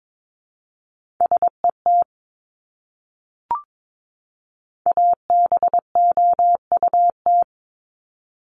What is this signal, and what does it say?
Morse code, heard as a keyed tone, about 22 wpm, 700 Hz. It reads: SET ABOUT
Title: Top 300 Words 22wpm